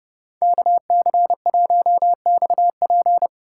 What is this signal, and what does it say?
Morse code, heard as a keyed tone, about 37 wpm, 700 Hz. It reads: KC1XP
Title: Speed Racing - USA Call Signs 30wpm